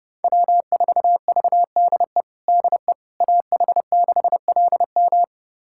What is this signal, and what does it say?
Morse code, heard as a keyed tone, about 30 wpm, 700 Hz. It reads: W4VDE DE AH6LM
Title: USA Callsign DE Callsign - 5 Characters - 30wpm